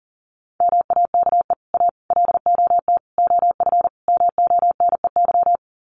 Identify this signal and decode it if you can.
Morse code, heard as a keyed tone, about 40 wpm, 700 Hz. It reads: MAKE A LOT OF MONEY
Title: Sets of 5 Words Encore2 40wpm